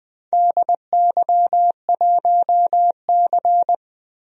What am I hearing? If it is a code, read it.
Morse code, heard as a keyed tone, about 20 wpm, 700 Hz. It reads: DY1C
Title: DXCC-100 Call Signs - 4 Characters - 20wpm